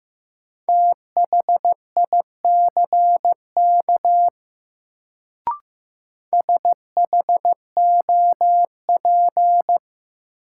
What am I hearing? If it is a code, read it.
Morse code, heard as a keyed tone, about 15 wpm, 700 Hz. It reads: THICK SHOP
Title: Sentences from Top 1600 Words - New Word Drill - 15wpm